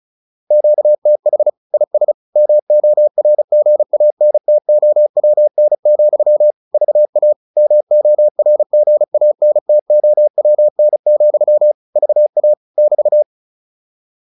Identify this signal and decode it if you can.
Morse code, heard as a keyed tone, about 34 wpm, 600 Hz. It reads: QTH IS MORGANTOWN, VA MORGANTOWN, VA <BT>